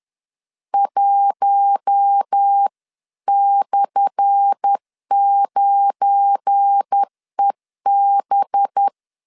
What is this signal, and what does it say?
Morse code, heard as a keyed tone, about 10 wpm, 800 Hz. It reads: 1/9EB
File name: LEZIONI / 10 SIGNS / 10A SLASH 10wpm 2808